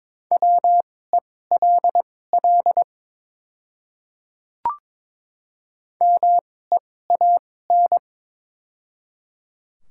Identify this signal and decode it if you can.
Morse code, heard as a keyed tone, about 22 wpm, spaced as for 16 wpm, 700 Hz. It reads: WELL MEAN
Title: 2x Character Spacing - Top 500 Words - 22wpm